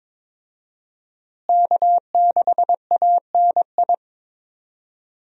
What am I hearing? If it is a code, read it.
Morse code, heard as a keyed tone, about 22 wpm, 700 Hz. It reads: K6ANI
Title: USA Call Signs - 5 Characters - Repeated 3x - 22wpm